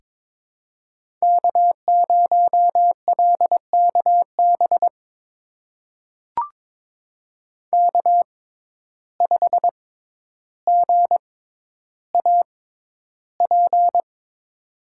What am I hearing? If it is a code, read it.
Morse code, heard as a keyed tone, about 23 wpm, spaced as for 11 wpm, 700 Hz. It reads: K0LKB K5GAP
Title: USA Call Signs - 5 Characters - 6x Character Spacing - 22wpm